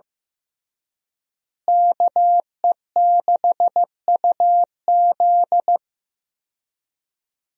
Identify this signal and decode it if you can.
Morse code, heard as a keyed tone, about 15 wpm, 700 Hz. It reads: KE6UZ
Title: USA Call Signs - 5 Characters - Repeated 3x - 15wpm